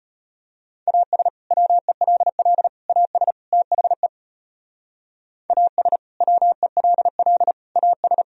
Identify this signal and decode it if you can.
Morse code, heard as a keyed tone, about 38 wpm, 700 Hz. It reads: AS WELL AS THE AS WELL AS